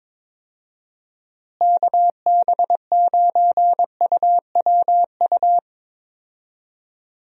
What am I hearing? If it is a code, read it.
Morse code, heard as a keyed tone, about 22 wpm, 700 Hz. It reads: KB9UWU
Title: USA Call Signs - 6 Characters - Repeated 2x - 22wpm